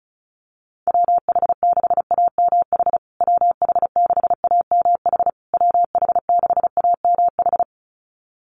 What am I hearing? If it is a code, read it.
Morse code, heard as a keyed tone, about 34 wpm, 700 Hz. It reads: WH6AMH WH6AMH WH6AMH
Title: USA Call Signs Repeated 3x 35wpm